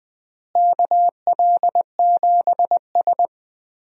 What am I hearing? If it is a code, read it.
Morse code, heard as a keyed tone, about 20 wpm, 700 Hz. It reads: KL7S
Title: DXCC-100 Call Signs - 4 Characters - Repeated 2x - 20wpm